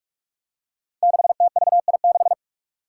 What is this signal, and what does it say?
Morse code, heard as a keyed tone, about 46 wpm, 700 Hz. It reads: 6TVI6
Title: Five Challenging Letter-Number Combinations 45wpm